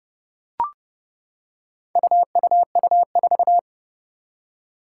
Morse code, heard as a keyed tone, about 30 wpm, 700 Hz. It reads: UUU4
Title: Four Challenging Letter-Number Combinations 30wpm